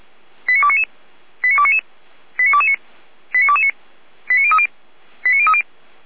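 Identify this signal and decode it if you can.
Five-tone selective call (ZVEI), heard as quick runs of five tones, tones of about 70 ms. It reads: 89299 89299 89229 89229 89030 89030
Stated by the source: ZVEI five-tone selective call 89337 received by SDR, logged by Feuerwehr Orlamünde.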